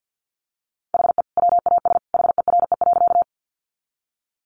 Morse code, heard as a keyed tone, about 50 wpm, 700 Hz. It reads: HE WAS HERE.
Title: Sentences from Top 200 Words 50wpm